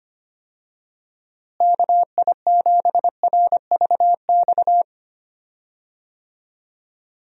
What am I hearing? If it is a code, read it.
Morse code, heard as a keyed tone, about 25 wpm, 700 Hz. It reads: KI7RVX
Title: USA Call Signs - 6 Characters - Repeated 2x - 25wpm